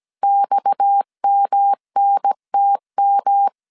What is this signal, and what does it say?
Morse code, heard as a keyed tone, about 17 wpm, 800 Hz. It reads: XMNTM
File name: CW / DET / 1 6 1500C S A 16wpm 2635